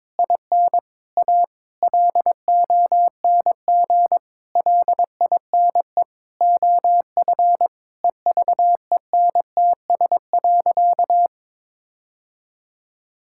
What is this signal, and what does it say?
Morse code, heard as a keyed tone, about 22 wpm, 700 Hz. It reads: IN A LONG LINE OF EVENTS.
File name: practice / 22wpm / Sentences with 8-Word Phrases - 22wpm